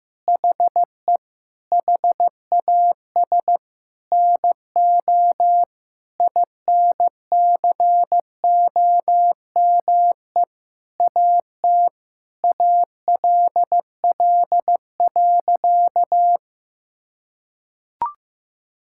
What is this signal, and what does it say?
Morse code, heard as a keyed tone, about 15 wpm, 700 Hz. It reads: HE HAS NO INCOME AT ALL.
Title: Sentences from Top 1000 Words - Review Part 5 - 15wpm